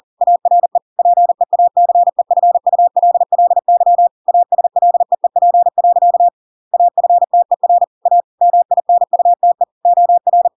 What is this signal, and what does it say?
Morse code, heard as a keyed tone, about 40 wpm, 700 Hz. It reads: ARE PEACEFULLY ASLEEP. AFTER A MINUTE OR
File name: Pinocchio / Pinocchio ch6 40 40 ik1hge